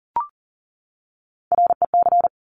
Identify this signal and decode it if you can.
Morse code, heard as a keyed tone, about 41 wpm, 700 Hz. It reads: REC
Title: Top Three Letter Combinations in Words 40wpm